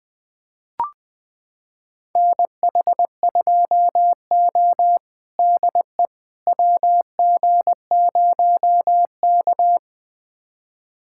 Morse code, heard as a keyed tone, about 20 wpm, 700 Hz. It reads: NH2O DE WG0K
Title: USA Callsign DE Callsign - 4 Characters - 20wpm